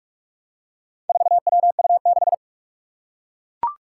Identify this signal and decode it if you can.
Morse code, heard as a keyed tone, about 43 wpm, 700 Hz. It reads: 4WU6
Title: Four Challenging Letter-Number Combinations 45wpm